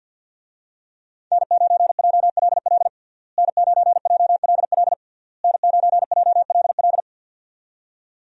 Morse code, heard as a keyed tone, about 52 wpm, 700 Hz. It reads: N9JLL N9JLL N9JLL
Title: USA Call Signs - 5 Characters - Repeated 3x - 50wpm